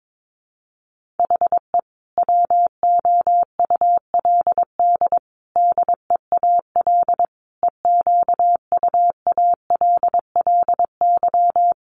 Morse code, heard as a keyed tone, about 22 wpm, 700 Hz. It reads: HE WOULD DEAL EQUALLY
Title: Sentences from Top 1400 Words 22wpm